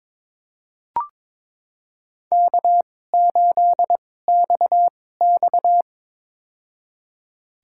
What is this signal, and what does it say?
Morse code, heard as a keyed tone, about 21 wpm, spaced as for 16 wpm, 700 Hz. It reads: K8XX
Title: USA Call Signs - 2x Character Spacing - 22wpm